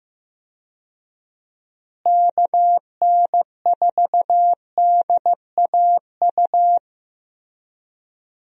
Morse code, heard as a keyed tone, about 15 wpm, 700 Hz. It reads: KN4DAU
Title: USA Call Signs - 6 Characters - Repeated 2x - 15wpm